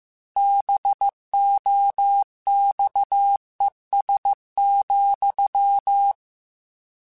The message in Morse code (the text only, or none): BOXES,